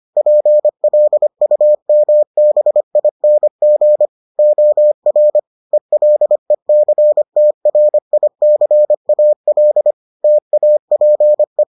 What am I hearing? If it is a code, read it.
Morse code, heard as a keyed tone, about 25 wpm, 600 Hz. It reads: PLUMBING OR ELECTRICAL TAPE